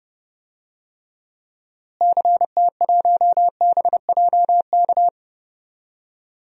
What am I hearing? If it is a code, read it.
Morse code, heard as a keyed tone, about 30 wpm, 700 Hz. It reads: CT1BJK
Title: DXCC-100 Call Signs - 6 Characters - Repeated 2x - 30wpm